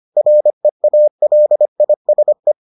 Morse code, heard as a keyed tone, about 25 wpm, 600 Hz. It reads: REALISE